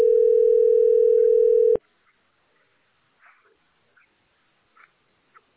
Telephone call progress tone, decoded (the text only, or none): ringback tone